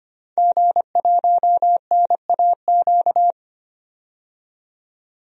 Morse code, heard as a keyed tone, about 25 wpm, 700 Hz. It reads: G1NAQ
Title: DXCC-100 Call Signs - 5 Characters - 25wpm